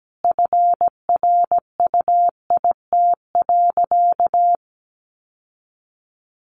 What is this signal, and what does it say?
Morse code, heard as a keyed tone, about 17 wpm, 700 Hz. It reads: FRUIT.
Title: Sentences from Top 1300 Words 17wpm